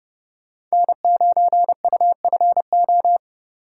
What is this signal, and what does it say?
Morse code, heard as a keyed tone, about 32 wpm, 700 Hz. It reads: N9UFO